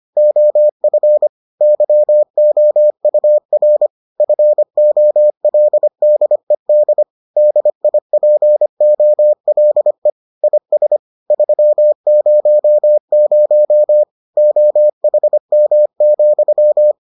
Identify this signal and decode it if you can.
Morse code, heard as a keyed tone, about 25 wpm, 600 Hz. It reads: OF YOUR FOLDED DIPOLE IS 300 OHM,